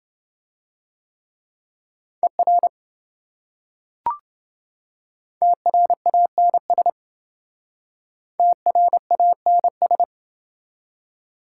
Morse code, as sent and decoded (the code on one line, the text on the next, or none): . .-. / - .-. .- -. ... / - .-. .- -. ...
ER TRANS TRANS